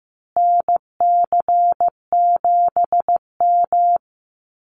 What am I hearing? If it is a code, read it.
Morse code, heard as a keyed tone, about 14 wpm, 700 Hz. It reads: NC7M
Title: Call Signs Repeated 3x - 15wpm